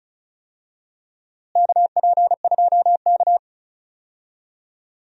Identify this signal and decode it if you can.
Morse code, heard as a keyed tone, about 36 wpm, 700 Hz. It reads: KP2K
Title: DXCC-100 Call Signs - 4 Characters - Repeated 2x - 35wpm